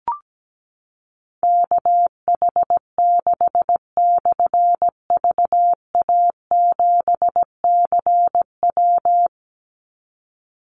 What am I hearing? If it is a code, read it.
Morse code, heard as a keyed tone, about 17 wpm, 700 Hz. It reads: KH6/VA7CW
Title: Call Signs - 17wpm